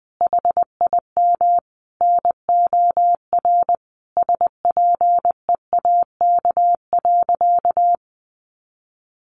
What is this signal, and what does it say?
Morse code, heard as a keyed tone, about 20 wpm, 700 Hz. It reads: HIM NOR SPEAK.
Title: Sentences from Top 1500 Words 20wpm